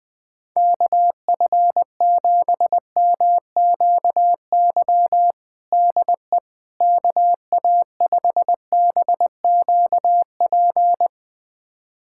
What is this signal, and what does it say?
Morse code, heard as a keyed tone, about 20 wpm, 700 Hz. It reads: KF7MQY DE KA5BQP
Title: USA Callsign DE Callsign - 6 Characters - 20wpm